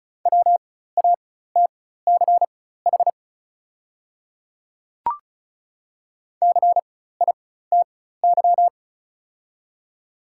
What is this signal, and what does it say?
Morse code, heard as a keyed tone, about 36 wpm, spaced as for 16 wpm, 700 Hz. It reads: WATCH CITY
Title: 4x Character Spacing - Top 500 Words - 35wpm